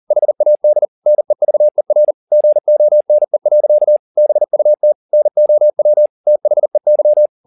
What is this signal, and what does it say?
Morse code, heard as a keyed tone, about 40 wpm, 600 Hz. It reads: HAD NEVER GONE. BUT NOW THEY
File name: the old man and the sea 1 40wpm